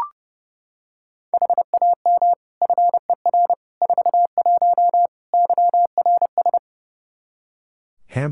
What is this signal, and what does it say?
Morse code, heard as a keyed tone, about 30 wpm, 700 Hz. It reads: HAM FER 41 YRS
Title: Practice QSO - Exchange 3 - More Info - 30wpm